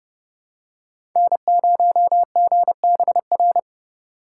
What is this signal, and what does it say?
Morse code, heard as a keyed tone, about 30 wpm, 700 Hz. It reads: N0GBR
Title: USA Call Signs - 5 Characters - 4x Character Spacing - 30wpm